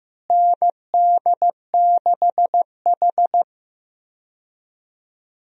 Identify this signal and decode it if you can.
Morse code, heard as a keyed tone, about 15 wpm, 700 Hz. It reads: ND6H
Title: USA Call Signs - 4 Characters - Repeated 2x - 15wpm